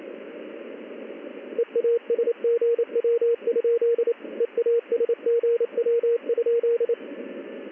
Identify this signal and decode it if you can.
Morse code, heard as a keyed tone, about 27 wpm, 470 Hz, with radio noise. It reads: EASGW? EASGW?
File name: contests / ss-ssb-k5tr- -1700